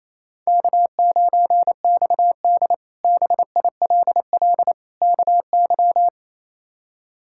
Morse code, heard as a keyed tone, about 28 wpm, 700 Hz. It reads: K9XD BILL KY
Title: K1USN SST QSO Practice - 28wpm